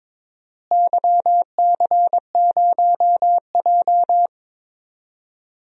Morse code, heard as a keyed tone, about 22 wpm, 700 Hz. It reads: YC0J
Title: DXCC-100 Call Signs - 4 Characters - Repeated 3x - 22wpm